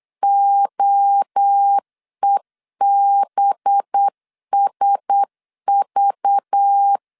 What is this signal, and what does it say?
Morse code, heard as a keyed tone, about 8 wpm, 800 Hz. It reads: OEBSV
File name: LEZIONI / 6 PKX0 / 6 P 8wpm 3025